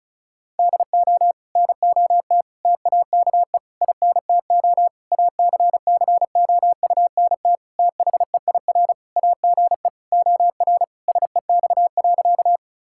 Morse code, heard as a keyed tone, about 34 wpm, 700 Hz. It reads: DO NOT TAKE INTO ACCOUNT THEIR AGE OR SEX.
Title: Sentences from Top 1600 Words 35wpm